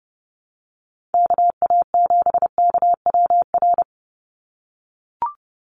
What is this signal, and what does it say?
Morse code, heard as a keyed tone, about 30 wpm, 700 Hz. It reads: KA7KWR
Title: USA Call Signs 30wpm